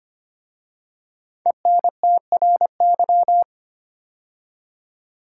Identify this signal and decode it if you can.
Morse code, heard as a keyed tone, about 26 wpm, 700 Hz. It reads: ENTRY
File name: 5 Letter Words - First 3 Letters Given - 25wpm